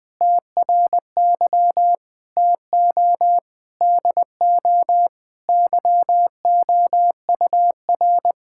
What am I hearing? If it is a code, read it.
Morse code, heard as a keyed tone, about 20 wpm, 700 Hz. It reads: TRY TO DO YOUR
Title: Sentences with 2-Word Phrases - Spoken Beginning - 20wpm